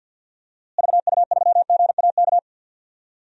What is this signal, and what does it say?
Morse code, heard as a keyed tone, about 48 wpm, 700 Hz. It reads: VU3GAK
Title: DXCC-100 Call Signs - 6 Characters - Repeated 2x - 50wpm